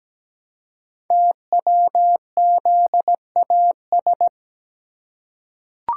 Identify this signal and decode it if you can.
Morse code, heard as a keyed tone, about 17 wpm, 700 Hz. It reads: TWZAS
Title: Code Groups - 5 Letters - 17wpm